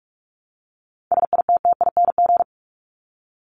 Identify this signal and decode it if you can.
Morse code, heard as a keyed tone, about 44 wpm, 700 Hz. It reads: SITTING
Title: ING Suffix - Top 100 Words with Base Words 45wpm